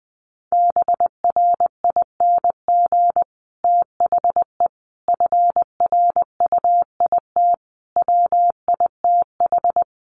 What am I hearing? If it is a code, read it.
Morse code, heard as a keyed tone, about 20 wpm, 700 Hz. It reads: BRING THE FRUIT WITH
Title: Sentences from Top 1300 Words 20wpm